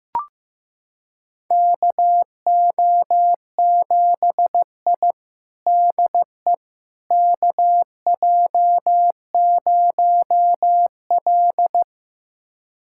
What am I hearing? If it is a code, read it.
Morse code, heard as a keyed tone, about 15 wpm, 700 Hz. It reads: KO7I DE KJ0L
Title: USA Callsign DE Callsign - 4 Characters - 15wpm